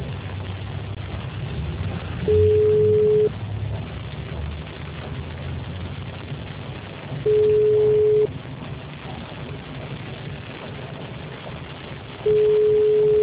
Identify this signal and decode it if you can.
Ringback tone (Europe, 425 Hz).
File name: out- -2014- - - .24264